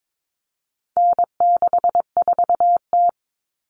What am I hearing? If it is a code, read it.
Morse code, heard as a keyed tone, about 22 wpm, 700 Hz. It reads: N64T
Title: Common Cut Numbers - 4-Digit 22wpm